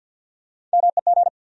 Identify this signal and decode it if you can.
Morse code, heard as a keyed tone, about 51 wpm, 700 Hz. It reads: MEG
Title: USA Common Names - Max 5 Letters - 50wpm